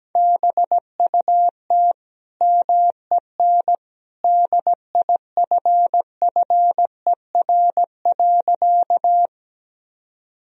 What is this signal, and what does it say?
Morse code, heard as a keyed tone, about 17 wpm, 700 Hz. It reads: BUT MEN DIFFER.
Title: Sentences from Top 1000 Words - Review Part 1 - 17wpm